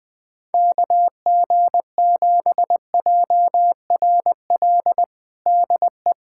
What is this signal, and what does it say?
Morse code, heard as a keyed tone, about 20 wpm, 700 Hz. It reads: KG7JRL DE
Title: USA Callsign DE Callsign - 6 Characters - 20wpm